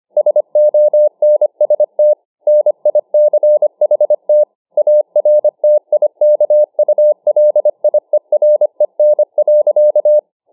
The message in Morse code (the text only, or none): SONST NICHT ARTIKULIEREN.